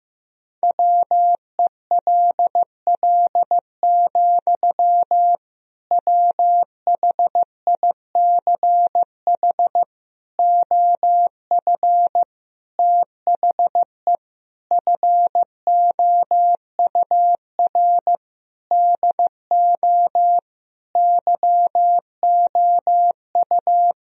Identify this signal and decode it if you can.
Morse code, heard as a keyed tone, about 15 wpm, 700 Hz. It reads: WELL, WHICH OF THE FOUR DO YOU
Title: Sentences from Top 1000 Words - Review Part 1 - 15wpm